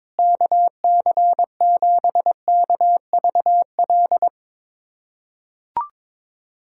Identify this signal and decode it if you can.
Morse code, heard as a keyed tone, about 22 wpm, 700 Hz. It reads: KC7KVL
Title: USA Callsign DE Callsign - 6 Characters - 22wpm